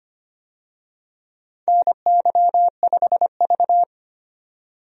Morse code, heard as a keyed tone, about 25 wpm, 700 Hz. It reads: NY5V
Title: USA Call Signs - 4 Characters - 25wpm